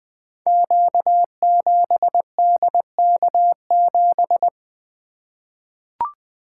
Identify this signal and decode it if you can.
Morse code, heard as a keyed tone, about 20 wpm, 700 Hz. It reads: Q7DK7
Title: Code Groups - 5 Letter:Numbers - 20wpm